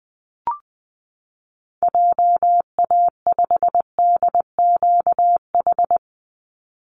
Morse code, heard as a keyed tone, about 20 wpm, 700 Hz. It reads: JA5DQH
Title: Call Signs Repeated 3x - 20wpm